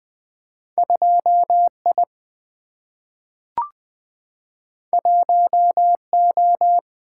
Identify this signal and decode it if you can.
Morse code, heard as a keyed tone, about 20 wpm, 700 Hz. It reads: 2I 1O
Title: Single-Letter Single-Number 20wpm